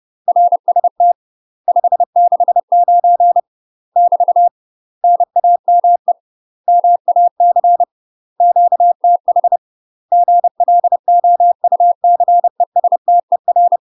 Morse code, heard as a keyed tone, about 30 wpm, 700 Hz. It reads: RST 569 <BT> NAME MAC QTH GLOUCESTER
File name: UK MASTER QSOs TRAINING / 30WPM / UK MASTER QSOs 0008